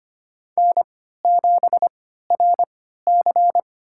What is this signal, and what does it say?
Morse code, heard as a keyed tone, about 25 wpm, spaced as for 14 wpm, 700 Hz. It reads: N7RC
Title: USA Call Signs - 3x Character Spacing - 25wpm